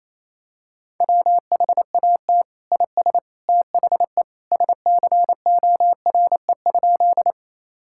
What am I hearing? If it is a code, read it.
Morse code, heard as a keyed tone, about 28 wpm, 700 Hz. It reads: WHAT IS THE SCORE?
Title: Sentences from Top 1600 Words 28wpm